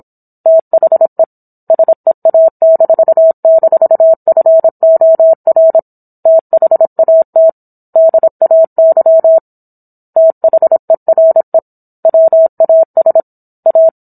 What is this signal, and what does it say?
Morse code, heard as a keyed tone, about 27 wpm, 650 Hz. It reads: THE SEA--FOR THAT DAY THERE WAS A